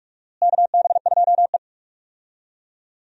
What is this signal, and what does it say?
Morse code, heard as a keyed tone, about 46 wpm, 700 Hz. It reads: KB2E
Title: USA Call Signs - 4 Characters - Repeated 3x - 45wpm